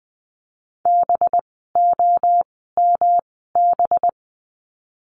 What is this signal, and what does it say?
Morse code, heard as a keyed tone, about 20 wpm, spaced as for 15 wpm, 700 Hz. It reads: BOMB
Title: 185 Lesson 30f - Letter b - Words 20wpm